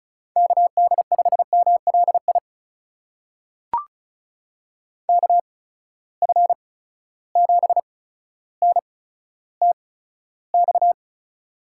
Morse code, heard as a keyed tone, about 34 wpm, spaced as for 15 wpm, 700 Hz. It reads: KD5MLI KF7NTX
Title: USA Call Signs - 8x Character Spacing - 35wpm